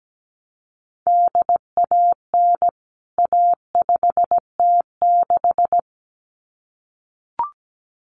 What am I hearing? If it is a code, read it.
Morse code, heard as a keyed tone, about 17 wpm, 700 Hz. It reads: DAN A5T6
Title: CWT Exchange 17wpm